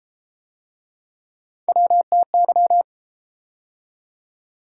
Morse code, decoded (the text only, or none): WTY